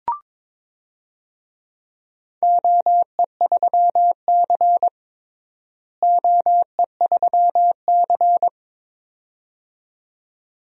Morse code, heard as a keyed tone, about 22 wpm, 700 Hz. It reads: OE3C OE3C
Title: DXCC-100 Call Signs - 4 Characters - Repeated 2x - 22wpm